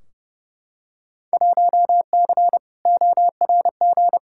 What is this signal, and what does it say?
Morse code, heard as a keyed tone, about 30 wpm, 700 Hz. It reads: 1C ORG
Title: Field Day Exchanges - 30wpm